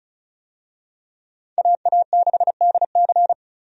Morse code, heard as a keyed tone, about 34 wpm, 700 Hz. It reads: AA6DC
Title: USA Call Signs - 5 Characters - 35wpm